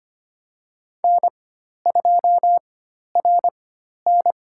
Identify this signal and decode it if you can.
Morse code, heard as a keyed tone, about 25 wpm, spaced as for 12 wpm, 700 Hz. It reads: N2RN
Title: USA Call Signs - 4 Characters - 4x Character Spacing - 25wpm